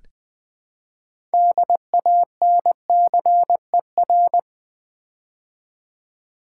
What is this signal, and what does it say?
Morse code, heard as a keyed tone, about 20 wpm, 700 Hz. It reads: DANCER
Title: Build A Word - 3 Letters At A Time - 6-Letter Words - 20wpm